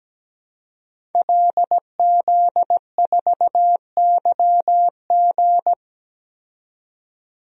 Morse code, heard as a keyed tone, about 17 wpm, 700 Hz. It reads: LZ4YG
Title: DXCC-100 Call Signs - 5 Characters - 17wpm